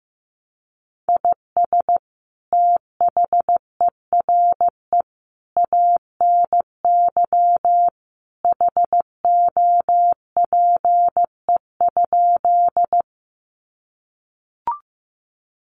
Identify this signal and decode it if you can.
Morse code, heard as a keyed tone, about 15 wpm, 700 Hz. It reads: IS THERE ANY HOPE?
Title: Sentences from Top 1000 Words 15wpm